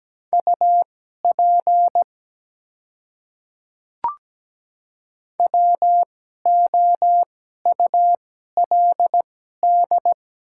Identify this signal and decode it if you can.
Morse code, heard as a keyed tone, about 17 wpm, spaced as for 12 wpm, 700 Hz. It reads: UP WOULD
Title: 2x Character Spacing - Top 300 Words - 17wpm